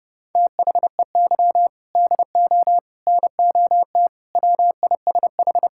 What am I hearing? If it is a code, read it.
Morse code, heard as a keyed tone, about 30 wpm, 700 Hz. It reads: THEY DO NOT WISH
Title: Sentences from Top 1600 Words 30wpm